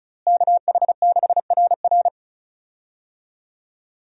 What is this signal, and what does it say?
Morse code, heard as a keyed tone, about 36 wpm, 700 Hz. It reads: KH6RR
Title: USA Call Signs - 5 Characters - Repeated 2x - 35wpm